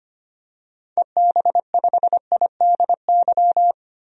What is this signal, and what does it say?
Morse code, heard as a keyed tone, about 25 wpm, 690 Hz. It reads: EB5IDY
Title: DXCC-100 Call Signs - 6 Characters - 25wpm